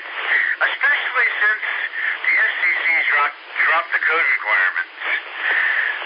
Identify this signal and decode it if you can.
Emergency Alert System (SAME) tones (header bursts), heard as harsh buzzing bursts, with speech. It is heard from 0.2 to 5.0 s.